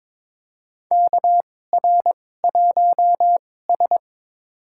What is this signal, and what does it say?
Morse code, heard as a keyed tone, about 22 wpm, spaced as for 16 wpm, 700 Hz. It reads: KR1S
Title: USA Call Signs - 2x Character Spacing - 22wpm